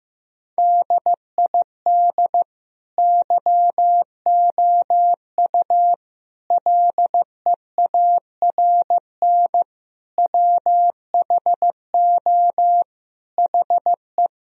Morse code, heard as a keyed tone, about 15 wpm, 700 Hz. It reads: DID YOU LEARN WHO HE
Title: Sentences from Top 1000 Words - Review Part 1 - 15wpm